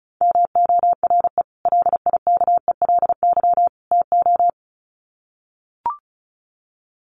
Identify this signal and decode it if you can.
Morse code, heard as a keyed tone, about 34 wpm, 700 Hz. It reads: MORE LIKELY TO
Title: Sets of 3 Words 35wpm